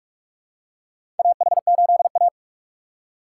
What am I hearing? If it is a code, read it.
Morse code, heard as a keyed tone, about 46 wpm, 700 Hz. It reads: AH8A
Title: USA Call Signs - 2x Character Spacing - 45wpm